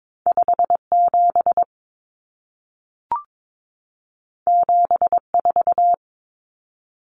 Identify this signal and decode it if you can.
Morse code, heard as a keyed tone, about 22 wpm, 700 Hz. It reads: 57 74
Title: Common Cut Numbers - 2-Digit 22wpm